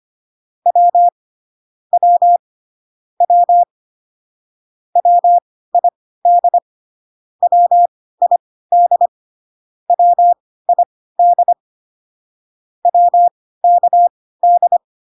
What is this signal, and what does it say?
Morse code, heard as a keyed tone, about 25 wpm, spaced as for 16 wpm, 700 Hz. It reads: W W W WID WID WID WKD